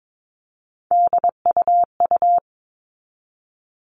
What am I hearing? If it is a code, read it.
Morse code, heard as a keyed tone, about 22 wpm, 700 Hz. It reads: DUU
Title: All Cut Numbers - 3-Digit 22wpm